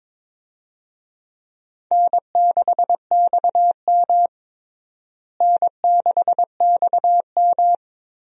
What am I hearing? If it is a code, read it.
Morse code, heard as a keyed tone, about 22 wpm, 700 Hz. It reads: N6XM N6XM
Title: USA Call Signs - 4 Characters - Repeated 2x - 22wpm